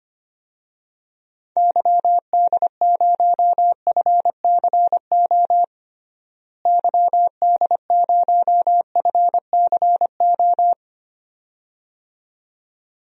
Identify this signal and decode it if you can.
Morse code, heard as a keyed tone, about 25 wpm, 700 Hz. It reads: YD0FCO YD0FCO
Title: DXCC-100 Call Signs - 6 Characters - Repeated 2x - 25wpm